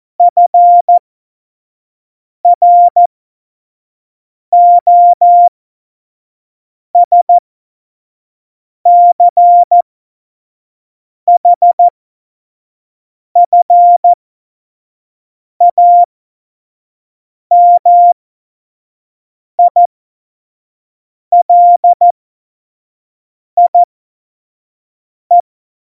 Morse code, heard as a keyed tone, about 14 wpm, spaced as for 5 wpm, 700 Hz. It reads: FROSCHFAMILIE
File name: Morsekurs / Witze / Witz 19. Juni 2018, 25 F70